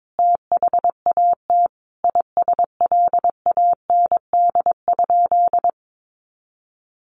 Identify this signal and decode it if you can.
Morse code, heard as a keyed tone, about 22 wpm, 700 Hz. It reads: THAT ISLAND?
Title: Sentences from Top 500 Words 22wpm